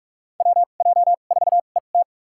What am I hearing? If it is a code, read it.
Morse code, heard as a keyed tone, about 46 wpm, spaced as for 33 wpm, 700 Hz. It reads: WJ4ET
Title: USA Call Signs - 2x Character Spacing - 45wpm